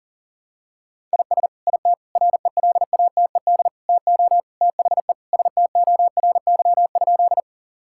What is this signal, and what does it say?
Morse code, heard as a keyed tone, about 40 wpm, 700 Hz. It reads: IS IT RELATED TO THE STORY?
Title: Sentences from Top 1000 Words - Review Part 5 - 40wpm